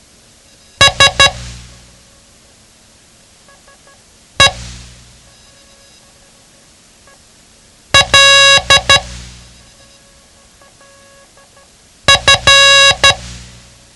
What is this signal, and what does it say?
Morse code, heard as a keyed tone, about 12 wpm, spaced as for 3 wpm, 1650 Hz. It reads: SELF
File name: ham radio / CodeQuick / Code Quick 07